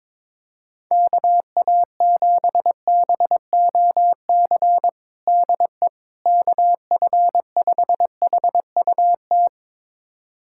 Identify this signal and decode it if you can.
Morse code, heard as a keyed tone, about 22 wpm, 700 Hz. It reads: KA7BOC DE KF5HUT
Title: USA Callsign DE Callsign - 6 Characters - 22wpm